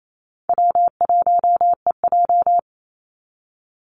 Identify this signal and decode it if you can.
Morse code, heard as a keyed tone, about 28 wpm, 700 Hz. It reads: W1EJ
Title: NA SOTA Activator Callsigns 28wpm